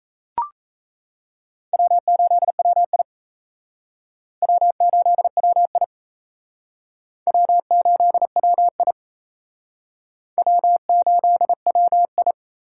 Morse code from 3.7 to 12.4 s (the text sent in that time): W8WI W8WI W8WI